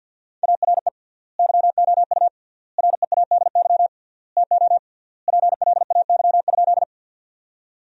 Morse code, heard as a keyed tone, about 52 wpm, 700 Hz. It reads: ARE YOU READY TO PLAY?
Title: 3x Word-Spacing - Sentences from Top 400 Words - 50wpm